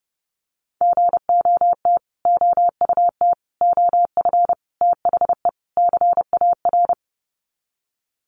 Morse code, heard as a keyed tone, about 30 wpm, 700 Hz. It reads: GOT OUT OF THE CAR
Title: Sets of 5 Words Encore 30wpm